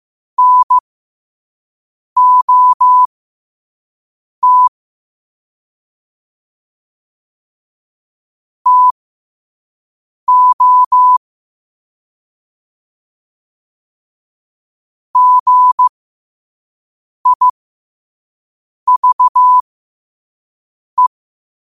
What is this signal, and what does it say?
Morse code, heard as a keyed tone, about 15 wpm, spaced as for 5 wpm, 1000 Hz. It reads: NOT TO GIVE
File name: qotdfiles / 5wpm